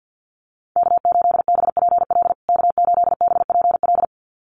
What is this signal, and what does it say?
Morse code, heard as a keyed tone, about 50 wpm, 700 Hz. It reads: K8BPL K8BPL
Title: USA Call Signs Repeated 2x 50wpm